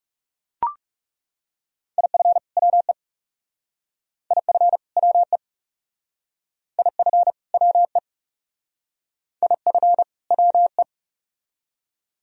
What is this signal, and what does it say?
Morse code, heard as a keyed tone, about 37 wpm, 700 Hz. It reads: IF WE IF WE IF WE IF WE
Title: Speed-Racing - Sets of 2 Words 30wpm